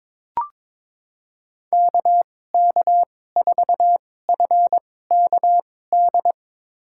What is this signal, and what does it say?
Morse code, heard as a keyed tone, about 22 wpm, spaced as for 16 wpm, 700 Hz. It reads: KK4FKD
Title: USA Call Signs - 2x Character Spacing - 22wpm